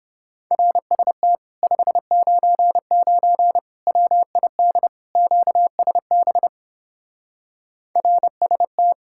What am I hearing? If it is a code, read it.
Morse code, heard as a keyed tone, about 30 wpm, 700 Hz. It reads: RST 599 WID QSB RST